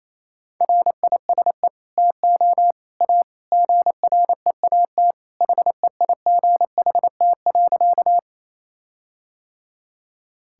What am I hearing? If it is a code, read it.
Morse code, heard as a keyed tone, about 28 wpm, 700 Hz. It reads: RISE TO A GREAT HEIGHT.
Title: Sentences from Top 1600 Words 28wpm